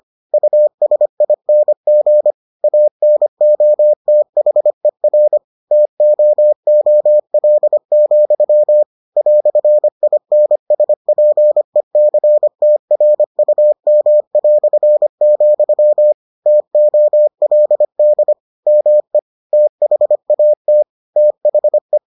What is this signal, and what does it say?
Morse code, heard as a keyed tone, about 25 wpm, 600 Hz. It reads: USING ANOTHER TOOL, "INSPECTRUM", TOLD ME THAT THE